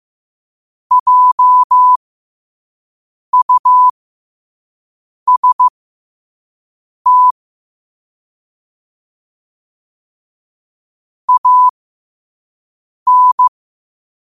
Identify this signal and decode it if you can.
Morse code, heard as a keyed tone, about 15 wpm, spaced as for 5 wpm, 1000 Hz. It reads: JUST AN
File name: qotdfiles / 5wpm